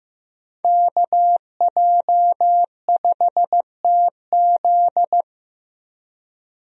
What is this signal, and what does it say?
Morse code, heard as a keyed tone, about 15 wpm, 700 Hz. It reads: KJ5TZ
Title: USA Call Signs - 5 Characters - 15wpm